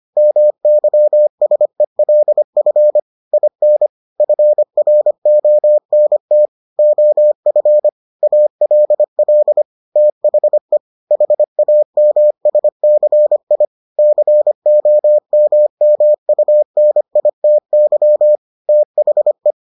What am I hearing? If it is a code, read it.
Morse code, heard as a keyed tone, about 25 wpm, 600 Hz. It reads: MYSELF IN FRONT OF ALL THE HAMSCI COMMUNITY THE